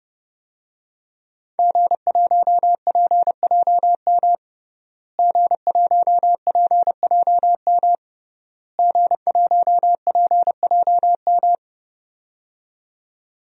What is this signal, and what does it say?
Morse code, heard as a keyed tone, about 30 wpm, 700 Hz. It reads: G1PJM G1PJM G1PJM
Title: DXCC-100 Call Signs - 5 Characters - Repeated 3x - 30wpm